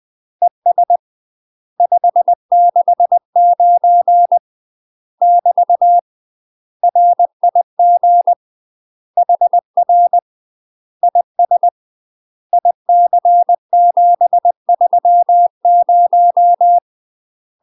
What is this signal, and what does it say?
Morse code, heard as a keyed tone, about 20 wpm, 700 Hz. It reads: ES 569 <BT> RIG HR IS IC730
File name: UK MASTER QSOs TRAINING / 20WPM / UK MASTER QSOs 0023